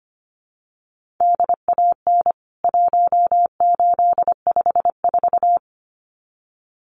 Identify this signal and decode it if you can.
Morse code, heard as a keyed tone, about 25 wpm, 700 Hz. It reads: DAN 1854
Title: CWT Exchange 25wpm